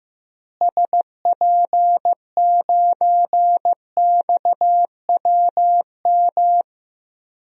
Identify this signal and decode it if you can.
Morse code, heard as a keyed tone, about 16 wpm, 700 Hz. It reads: SP9XWM
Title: DXCC-100 Call Signs - 6 Characters - Repeated 3x - 15wpm